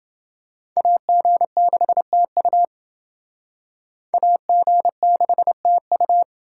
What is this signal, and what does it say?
Morse code, heard as a keyed tone, about 28 wpm, 700 Hz. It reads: AG6TU AG6TU